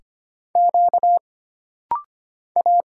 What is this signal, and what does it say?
Morse code, heard as a keyed tone, about 25 wpm, 700 Hz. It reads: Q A
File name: Single Character (Letter:Number:Punctuation:Prosign) - Rapid-Fire - 25wpm